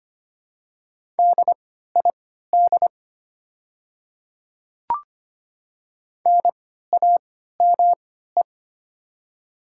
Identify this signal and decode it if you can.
Morse code, heard as a keyed tone, about 25 wpm, spaced as for 14 wpm, 700 Hz. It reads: DID NAME
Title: 3x Character Spacing - Top 300 Words - 25wpm